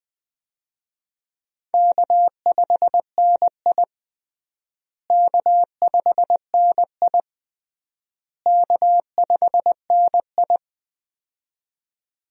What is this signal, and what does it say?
Morse code, heard as a keyed tone, about 20 wpm, 700 Hz. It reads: K5NI K5NI K5NI
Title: USA Call Signs - 4 Characters - Repeated 3x - 20wpm